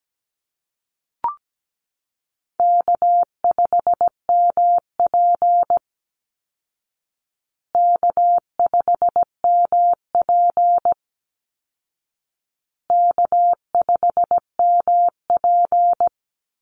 Morse code, heard as a keyed tone, about 17 wpm, 700 Hz. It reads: K5MP K5MP K5MP
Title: Call Signs Repeated 2x - 17wpm